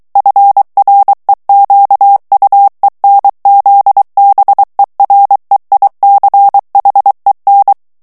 Morse code, heard as a keyed tone, about 23 wpm, 800 Hz. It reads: FREQUENZBEREICHEN
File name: ALLEMAND texte clair 1 vitesse 22 mots minute